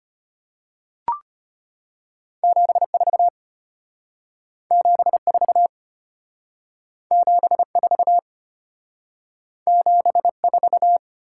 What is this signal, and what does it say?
Morse code, heard as a keyed tone, about 33 wpm, 700 Hz. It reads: 74 74 74 74